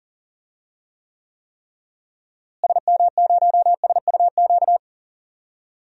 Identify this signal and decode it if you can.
Morse code, heard as a keyed tone, about 40 wpm, 700 Hz. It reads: SM0SUQ
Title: DXCC-100 Call Signs - 6 Characters - Repeated 3x - 40wpm